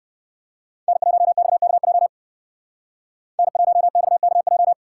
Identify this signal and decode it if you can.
Morse code, heard as a keyed tone, about 60 wpm, 700 Hz. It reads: N1XKJ N1XKJ